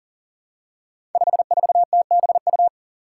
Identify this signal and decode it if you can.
Morse code, heard as a keyed tone, about 40 wpm, 700 Hz. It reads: 54TBU
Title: Five Challenging Letter-Number Combinations 40wpm